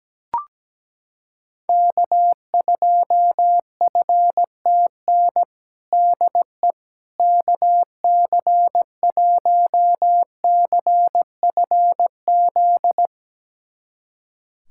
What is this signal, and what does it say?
Morse code, heard as a keyed tone, about 17 wpm, 700 Hz. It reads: K2FTN DE KC1CFZ
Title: USA Callsign DE Callsign 17wpm